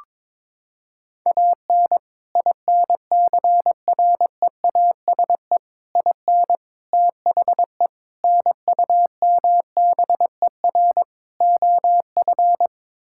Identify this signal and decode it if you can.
Morse code, heard as a keyed tone, about 22 wpm, 700 Hz. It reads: AN INCREASE IN THE NUMBER OF
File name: practice / 22wpm / Sets of 6 Words - 22wpm